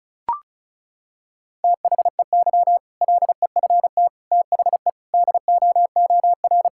Morse code, heard as a keyed tone, about 36 wpm, 700 Hz. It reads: THEY LEFT THE DOOR
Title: Sentences from Top 1000 Words - Review Part 6 - 35wpm